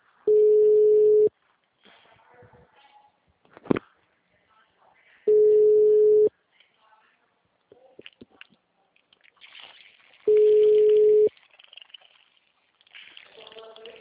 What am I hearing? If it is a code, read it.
Ringback tone (Europe, 425 Hz).
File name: out- -2007- - - .85741